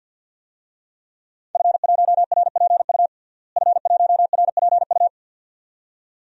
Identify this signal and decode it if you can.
Morse code, heard as a keyed tone, about 52 wpm, 700 Hz. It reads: F1RPU F1RPU
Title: DXCC-100 Call Signs - 5 Characters - Repeated 2x - 50wpm